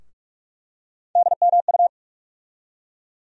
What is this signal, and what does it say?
Morse code, heard as a keyed tone, about 44 wpm, 700 Hz. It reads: DMU
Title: USA Call Sign Postfixes - 3 Letters 45wpm